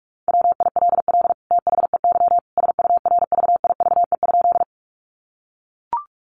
Morse code, heard as a keyed tone, about 44 wpm, 700 Hz. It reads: WILL THEY SURVIVE?
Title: Sentences from Top 1300 Words 45wpm